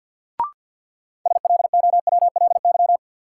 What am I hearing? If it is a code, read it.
Morse code, heard as a keyed tone, about 48 wpm, 700 Hz. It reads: SLOWLY
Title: Sentences from Top 1100 Words - New Word Drill - 50wpm